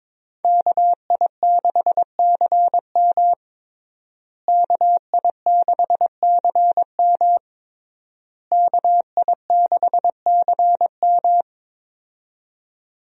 Morse code, heard as a keyed tone, about 21 wpm, 700 Hz. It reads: KI6CM KI6CM KI6CM
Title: USA Call Signs - 5 Characters - Repeated 3x - 22wpm